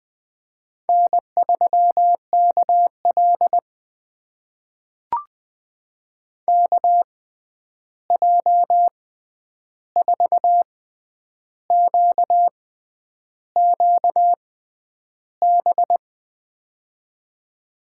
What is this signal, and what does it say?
Morse code, heard as a keyed tone, about 20 wpm, spaced as for 9 wpm, 700 Hz. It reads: N3KL KJ4QQB
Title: USA Call Signs - 6x Character Spacing - 20wpm